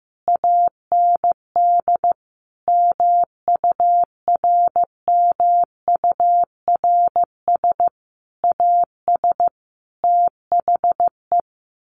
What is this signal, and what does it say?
Morse code, heard as a keyed tone, about 15 wpm, 700 Hz. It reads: AND MURMURS AS THE
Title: A Sound Of Thunder 15wpm